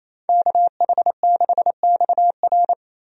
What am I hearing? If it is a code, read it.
Morse code, heard as a keyed tone, about 28 wpm, 700 Hz. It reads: KH6XR
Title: DXCC-100 Call Signs - 5 Characters - Repeated 3x - 28wpm